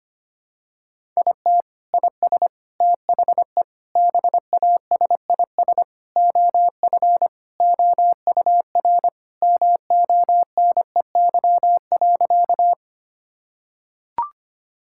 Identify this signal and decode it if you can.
Morse code, heard as a keyed tone, about 25 wpm, 700 Hz. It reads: IT IS THE BASIS OF OUR MONEY.
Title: Sentences from Top 1000 Words - Review Part 5 - 25wpm